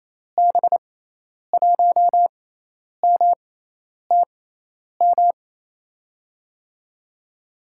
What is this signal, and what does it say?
Morse code, heard as a keyed tone, about 27 wpm, spaced as for 10 wpm, 700 Hz. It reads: B1MTM
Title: USA Call Signs - 6 Characters - 6x Character Spacing - 28wpm